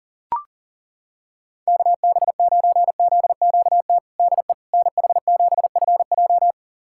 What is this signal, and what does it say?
Morse code, heard as a keyed tone, about 40 wpm, 700 Hz. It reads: KB9ZQT DE NH7FJ
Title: USA Callsign DE Callsign 40wpm